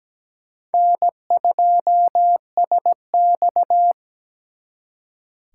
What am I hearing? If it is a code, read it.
Morse code, heard as a keyed tone, about 20 wpm, 700 Hz. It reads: N2SX
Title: Speed Racing - USA 1x2 Call Signs 17wpm